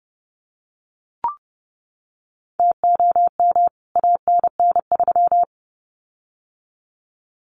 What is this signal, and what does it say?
Morse code, heard as a keyed tone, about 30 wpm, 700 Hz. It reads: TOM ANN3
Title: CWT Exchange 30wpm